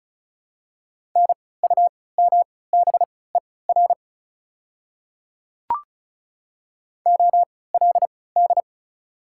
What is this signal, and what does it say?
Morse code, heard as a keyed tone, about 34 wpm, spaced as for 20 wpm, 700 Hz. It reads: NUMBER OLD
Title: 3x Character Spacing - Top 300 Words - 35wpm